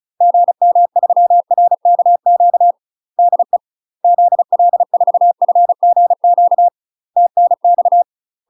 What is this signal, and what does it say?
Morse code, heard as a keyed tone, about 34 wpm, 700 Hz. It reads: GM3RKQ DE ZL4FGQ TNX
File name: UK MASTER QSOs TRAINING / 35WPM / UK MASTER QSOs 0000